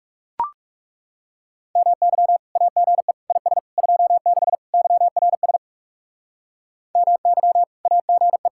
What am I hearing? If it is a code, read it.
Morse code, heard as a keyed tone, about 37 wpm, 700 Hz. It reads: MY AGE IS 26 YRS MY AGE
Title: Speed-Racing Common QSO Phrases - 30wpm